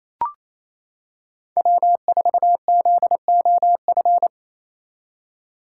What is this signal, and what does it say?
Morse code, heard as a keyed tone, about 27 wpm, 700 Hz. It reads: W4ZOF
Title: Code Groups - 5 Letter:Numbers - 28wpm